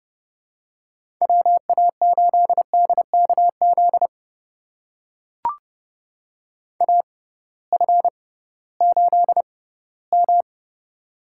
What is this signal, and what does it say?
Morse code, heard as a keyed tone, about 32 wpm, spaced as for 18 wpm, 700 Hz. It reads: WA8DKZ AF8M
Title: USA Call Signs - 6x Character Spacing - 30wpm